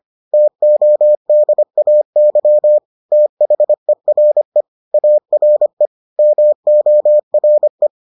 Morse code, heard as a keyed tone, about 25 wpm, 600 Hz. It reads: TODAY THERE ARE MORE